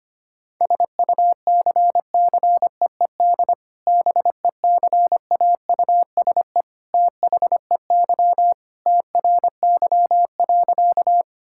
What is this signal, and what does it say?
Morse code, heard as a keyed tone, about 25 wpm, 700 Hz. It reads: SUCCEED BECAUSE THEY TRY.
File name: practice / 25wpm / Sentences with 2-Word Phrases - 25wpm